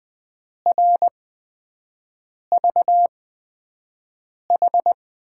Morse code, heard as a keyed tone, about 20 wpm, spaced as for 5 wpm, 700 Hz. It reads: RVH
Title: USA Call Signs - 5 Characters - 8x Character Spacing - 20wpm